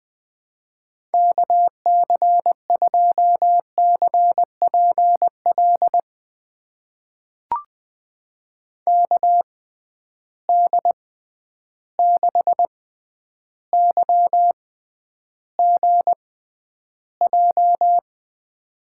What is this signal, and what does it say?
Morse code, heard as a keyed tone, about 20 wpm, spaced as for 10 wpm, 700 Hz. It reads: KC2CPL KD6YGJ
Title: USA Call Signs - 6 Characters - 6x Character Spacing - 20wpm